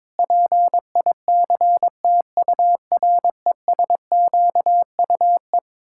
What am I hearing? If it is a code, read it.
Morse code, heard as a keyed tone, about 22 wpm, 700 Hz. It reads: PICTURESQUE
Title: Pangram Challenge v2 - 22wpm